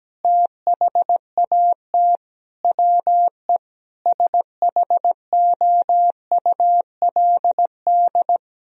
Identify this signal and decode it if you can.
Morse code, heard as a keyed tone, about 17 wpm, 700 Hz. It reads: THAT WE SHOULD
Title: Sentences from Top 1000 Words - Review Part 4 - 17wpm